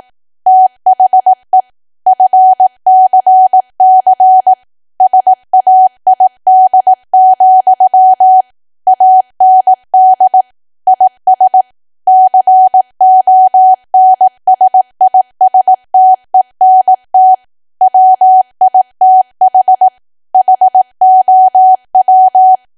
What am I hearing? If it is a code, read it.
Morse code, heard as a keyed tone, about 18 wpm, 750 Hz. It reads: THE FCC SAID, AND IS CONSISTENT WITH HOW
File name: cwa / W1AW-Level2 / 18WPM